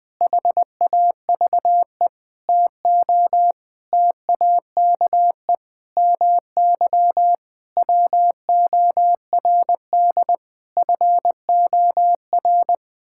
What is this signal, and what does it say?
Morse code, heard as a keyed tone, about 20 wpm, 700 Hz. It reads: HAVE TO TAKE MY WORD FOR
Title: Sets of 8 Words - 20wpm